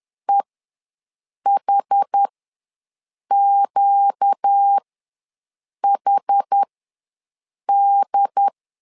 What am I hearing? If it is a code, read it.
Morse code, heard as a keyed tone, about 10 wpm, spaced as for 6 wpm, 800 Hz. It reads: EHQHD
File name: LEZIONI / 4 Q ws 10wpm 2306